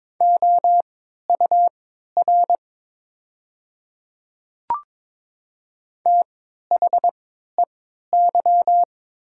Morse code, heard as a keyed tone, about 22 wpm, spaced as for 13 wpm, 700 Hz. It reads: OUR THEY
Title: 3x Character Spacing - Top 200 Words - 22wpm